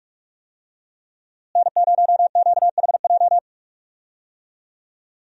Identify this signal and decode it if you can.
Morse code, heard as a keyed tone, about 46 wpm, 700 Hz. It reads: N0QHJ
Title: USA Call Signs - 5 Characters - Repeated 2x - 45wpm